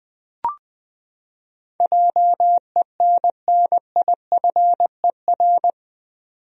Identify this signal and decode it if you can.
Morse code, heard as a keyed tone, about 21 wpm, 700 Hz. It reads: JENNIFER
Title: USA Common Names - 20wpm